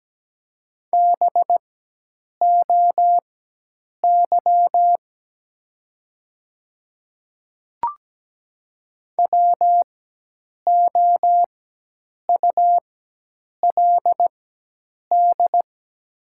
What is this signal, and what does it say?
Morse code, heard as a keyed tone, about 17 wpm, spaced as for 8 wpm, 700 Hz. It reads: BOY WOULD
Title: 4x Character Spacing - Top 300 Words - 17wpm